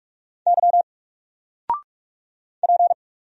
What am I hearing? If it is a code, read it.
Morse code, heard as a keyed tone, about 44 wpm, 700 Hz. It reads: Y P
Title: Single Letter-Number 45wpm